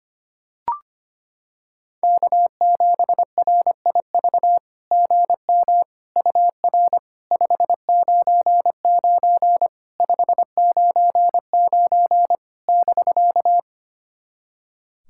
Morse code, heard as a keyed tone, about 25 wpm, 690 Hz. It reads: K7RIV GM UR 599 599 <BK>
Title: SOTA QSO Practice 25wpm